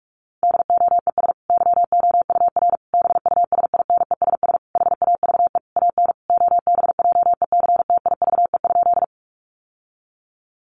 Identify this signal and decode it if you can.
Morse code, heard as a keyed tone, about 44 wpm, 700 Hz. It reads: DOES YOUR BUSINESS HAVE AN OBJECTIVE?
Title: Sentences from Top 1200 Words 45wpm